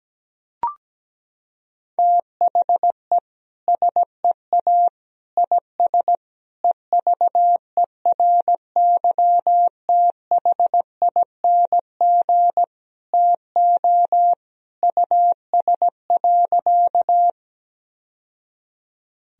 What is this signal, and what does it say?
Morse code, heard as a keyed tone, about 17 wpm, 700 Hz. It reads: THE SEA IS EVERYTHING TO US.
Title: Sentences from Top 1000 Words - Review Part 1 - 17wpm